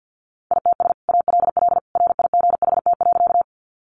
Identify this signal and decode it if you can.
Morse code, heard as a keyed tone, about 50 wpm, 700 Hz. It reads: ITS ALL RIGHT.
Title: Sentences from Top 600 Words 50wpm